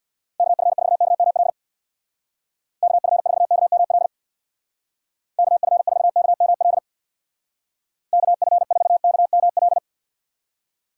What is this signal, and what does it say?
Morse code, heard as a keyed tone, about 65 wpm, 700 Hz. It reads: KF4KML KF4KML KF4KML KF4KML